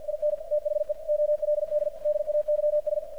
Morse code, heard as a keyed tone, about 51 wpm, 600 Hz, with radio noise. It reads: IN THE OLD DAYS
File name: archived / CRSnet